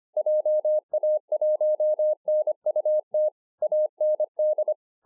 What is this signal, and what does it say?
Morse code, heard as a keyed tone, about 24 wpm, 610 Hz. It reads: JA1NUT AND